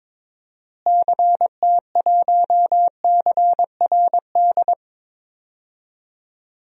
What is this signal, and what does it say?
Morse code, heard as a keyed tone, about 22 wpm, 700 Hz. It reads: CT1CRD
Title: DXCC-100 Call Signs - 6 Characters - Repeated 2x - 22wpm